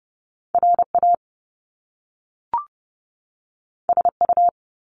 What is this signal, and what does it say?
Morse code, heard as a keyed tone, about 30 wpm, 700 Hz. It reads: RA SU
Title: Top Two Letter Combinations in Words 30wpm